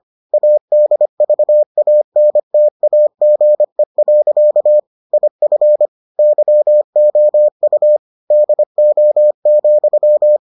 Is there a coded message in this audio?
Yes: Morse code from 0.3 to 10.4 s, heard as a keyed tone, about 25 wpm, 600 Hz. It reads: ADVANTAGE. IF YOU DO,